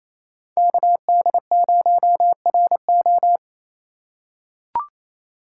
Morse code, heard as a keyed tone, about 27 wpm, 700 Hz. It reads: KD0RO
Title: USA Call Signs - 5 Characters - 4x Character Spacing - 28wpm